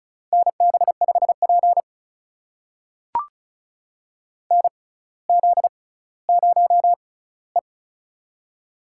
Morse code, heard as a keyed tone, about 36 wpm, spaced as for 18 wpm, 700 Hz. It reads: NB5P NZ0E
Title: USA Call Signs - 4 Characters - 6x Character Spacing - 35wpm